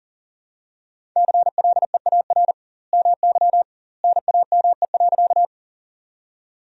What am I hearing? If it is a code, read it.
Morse code, heard as a keyed tone, about 40 wpm, 700 Hz. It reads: CLEAR MY NAME.
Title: 2x Word-Spacing - Sentences from Top 500 Words - 40wpm